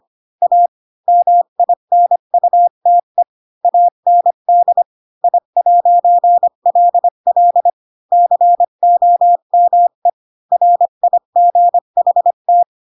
Morse code, heard as a keyed tone, about 25 wpm, 700 Hz. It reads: A MINUTE AND I'LL COME RIGHT